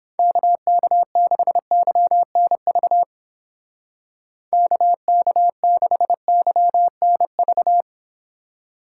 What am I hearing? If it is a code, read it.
Morse code, heard as a keyed tone, about 29 wpm, 700 Hz. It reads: KK6YNV KK6YNV